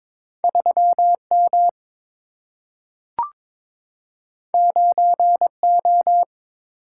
Morse code, heard as a keyed tone, about 22 wpm, 700 Hz. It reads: 3 M 9 O
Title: Single-Letter Single-Number 22wpm